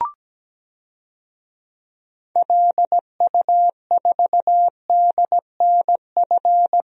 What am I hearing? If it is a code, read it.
Morse code, heard as a keyed tone, about 17 wpm, 700 Hz. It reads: LU4DNF
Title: DXCC-100 Call Signs - 6 Characters - 17wpm